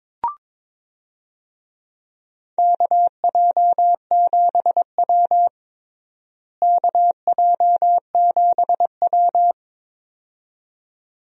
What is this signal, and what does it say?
Morse code, heard as a keyed tone, about 22 wpm, 700 Hz. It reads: KJ7W KJ7W
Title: USA Call Signs - 4 Characters - Repeated 2x - 22wpm